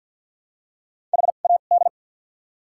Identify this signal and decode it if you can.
Morse code, heard as a keyed tone, about 48 wpm, spaced as for 35 wpm, 700 Hz. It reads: HAD
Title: 2x Character Spacing - Top 200 Words - 50wpm